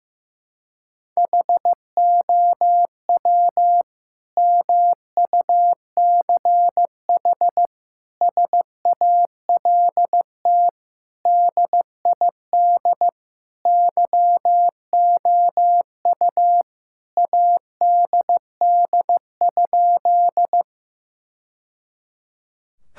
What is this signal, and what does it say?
Morse code, heard as a keyed tone, about 16 wpm, 700 Hz. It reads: HOW MUCH SALT DID YOU ADD?
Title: Sentences from Top 1000 Words - Review Part 5 - 15wpm